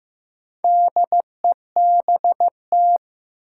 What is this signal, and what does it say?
Morse code, heard as a keyed tone, about 15 wpm, 700 Hz. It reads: DEBT
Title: Sentences from Top 1300 Words - New Word Drill - 15wpm